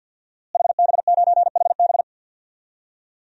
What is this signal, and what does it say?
Morse code, heard as a keyed tone, about 46 wpm, 700 Hz. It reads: HB9HB
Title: DXCC-100 Call Signs - 5 Characters - Repeated 2x - 50wpm